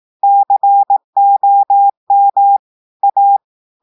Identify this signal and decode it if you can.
Morse code, heard as a keyed tone, about 18 wpm, 800 Hz. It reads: COM A
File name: audios / portugais / portugais texte en clair 1 vitesse 18 mots minutes